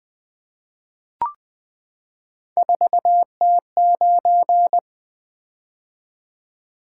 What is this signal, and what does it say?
Morse code, heard as a keyed tone, about 20 wpm, 700 Hz. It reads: 4T9
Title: Three Letter-Number Combinations - 20wpm